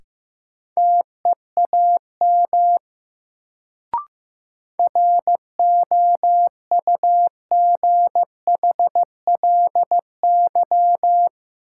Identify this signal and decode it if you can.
Morse code, heard as a keyed tone, about 15 wpm, 700 Hz. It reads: TEAM ROUGHLY
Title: Sentences from Top 1500 Words - New Word Drill Alt - 15wpm